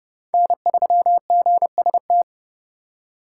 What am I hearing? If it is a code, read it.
Morse code, heard as a keyed tone, about 30 wpm, 700 Hz. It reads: N3GST
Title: USA Call Signs - 5 Characters - 3x Character Spacing - 30wpm